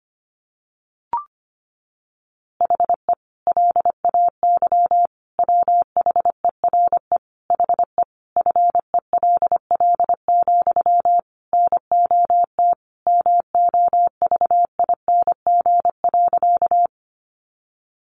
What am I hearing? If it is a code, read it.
Morse code, heard as a keyed tone, about 25 wpm, 700 Hz. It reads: HE LAY WHERE HE FELL, NOT MOVING.
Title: A Sound Of Thunder 25wpm